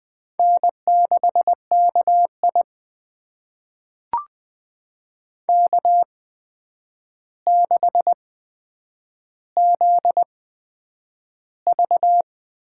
Morse code, heard as a keyed tone, about 21 wpm, spaced as for 9 wpm, 700 Hz. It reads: N6KI K6ZV
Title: USA Call Signs - 4 Characters - 8x Character Spacing - 20wpm